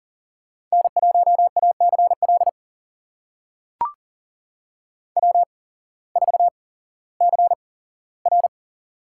Morse code, heard as a keyed tone, about 40 wpm, spaced as for 19 wpm, 700 Hz. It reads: N1ACL W4CR
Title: USA Call Signs - 8x Character Spacing - 40wpm